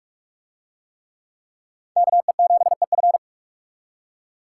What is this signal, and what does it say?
Morse code, heard as a keyed tone, about 38 wpm, 700 Hz. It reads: KE7EF